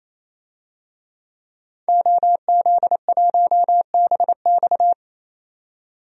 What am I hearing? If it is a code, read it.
Morse code, heard as a keyed tone, about 28 wpm, 700 Hz. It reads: OZ1BX
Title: DXCC-100 Call Signs - 5 Characters - 28wpm